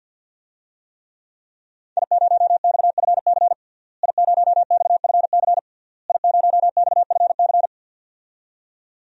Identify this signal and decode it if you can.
Morse code, heard as a keyed tone, about 53 wpm, 700 Hz. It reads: I0XFC I0XFC I0XFC
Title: DXCC-100 Call Signs - 5 Characters - Repeated 3x - 50wpm